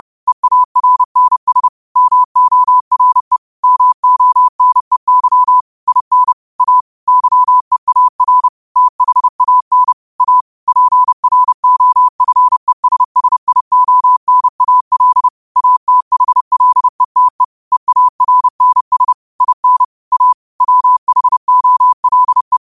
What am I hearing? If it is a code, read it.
Morse code, heard as a keyed tone, about 30 wpm, 1000 Hz. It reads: EARNS MORE MONEY IN A YEAR THAN A PROFESSIONAL ATHLETE EARNS IN A WHOLE
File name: qotdfiles / 30wpm